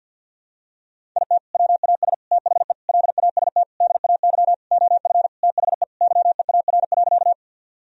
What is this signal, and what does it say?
Morse code, heard as a keyed tone, about 52 wpm, 700 Hz. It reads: IT WAS THE LAST DAY OF THE YEAR.
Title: Sentences from Top 1000 Words - Review Part 1 - 50wpm